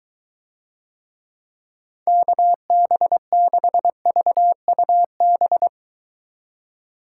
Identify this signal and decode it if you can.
Morse code, heard as a keyed tone, about 23 wpm, 700 Hz. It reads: KB6VUB